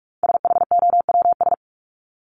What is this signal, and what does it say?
Morse code, heard as a keyed tone, about 44 wpm, 700 Hz. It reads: SHOWS
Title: ES:S Suffix - Top 100 Words with Base Words 45wpm